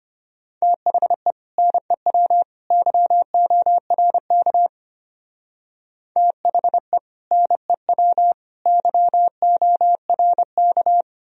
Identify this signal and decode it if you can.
Morse code, heard as a keyed tone, about 32 wpm, 700 Hz. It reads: THE NEW YORK THE NEW YORK